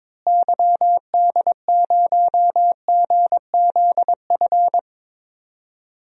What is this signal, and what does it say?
Morse code, heard as a keyed tone, about 22 wpm, 700 Hz. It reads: YD0GZF
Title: DXCC-100 Call Signs - 6 Characters - Repeated 3x - 22wpm